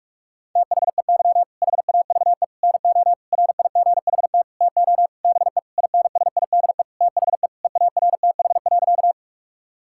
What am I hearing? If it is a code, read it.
Morse code, heard as a keyed tone, about 46 wpm, 700 Hz. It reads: THEY HAVE NO RIGHT TO BE INSIDE THE EARTH.
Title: Sentences from Top 1000 Words - Review Part 6 - 45wpm